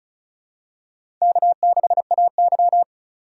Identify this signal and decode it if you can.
Morse code, heard as a keyed tone, about 36 wpm, 700 Hz. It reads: K6AY
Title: USA Call Signs - 4 Characters - 4x Character Spacing - 35wpm